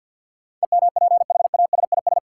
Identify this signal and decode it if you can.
Morse code, heard as a keyed tone, about 49 wpm, 700 Hz. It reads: EMPHASIS
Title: Sentences from Top 1400 Words - New Word Drill Alt - 50wpm